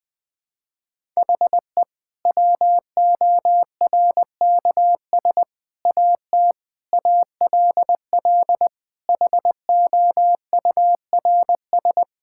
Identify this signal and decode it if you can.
Morse code, heard as a keyed tone, about 21 wpm, 700 Hz. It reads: HE WORKS AT ALL HOURS
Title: Sentences with 8-Word Phrases - Spoken Beginning - 20wpm